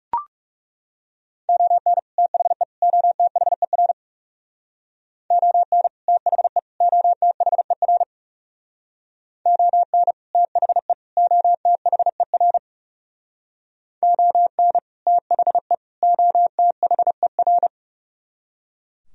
Morse code, heard as a keyed tone, about 35 wpm, 700 Hz. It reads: ON THE OTHER ON THE OTHER ON THE OTHER ON THE OTHER
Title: Speed-Racing - Sets of 3 Words 30wpm